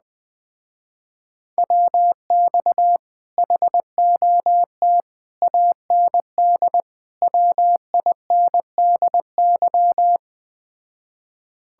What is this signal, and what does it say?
Morse code, heard as a keyed tone, about 20 wpm, 700 Hz. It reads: WX HOT AND WINDY
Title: Speed-Racing Common QSO Phrases - 17wpm